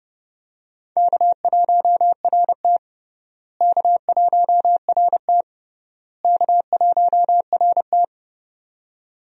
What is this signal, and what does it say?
Morse code, heard as a keyed tone, about 32 wpm, 700 Hz. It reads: K1RT K1RT K1RT
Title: USA Call Signs - 4 Characters - Repeated 3x - 30wpm